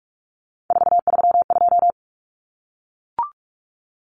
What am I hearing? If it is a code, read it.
Morse code, heard as a keyed tone, about 44 wpm, 700 Hz. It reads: 432
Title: Common Cut Numbers - 3-Digit 45wpm